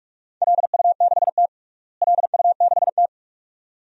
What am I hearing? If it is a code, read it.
Morse code, heard as a keyed tone, about 46 wpm, 700 Hz. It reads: LU6T LU6T
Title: DXCC-100 Call Signs - 4 Characters - Repeated 2x - 45wpm